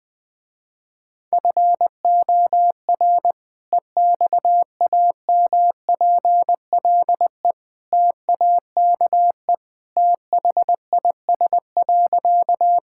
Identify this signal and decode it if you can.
Morse code, heard as a keyed tone, about 20 wpm, 700 Hz. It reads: FOR EXAMPLE TAKE THIS.
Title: Sentences with 2-Word Phrases - Spoken Beginning - 20wpm